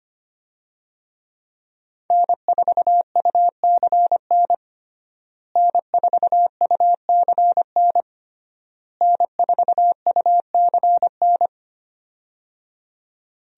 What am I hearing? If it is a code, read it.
Morse code, heard as a keyed tone, about 25 wpm, 700 Hz. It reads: N4UCN N4UCN N4UCN
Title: USA Call Signs - 5 Characters - Repeated 3x - 25wpm